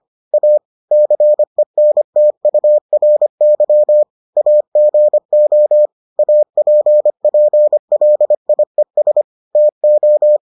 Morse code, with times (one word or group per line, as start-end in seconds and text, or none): 0.3-0.6 A
0.9-4.0 CENTURY
4.4-5.9 AGO
6.2-9.2 APPLIES
9.6-10.4 TO